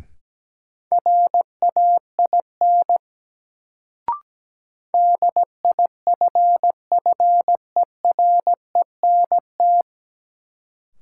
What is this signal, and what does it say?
Morse code, heard as a keyed tone, about 17 wpm, 700 Hz. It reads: RAIN DIFFERENT
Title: Sentences from Top 600 Words - New Word Drill - 17wpm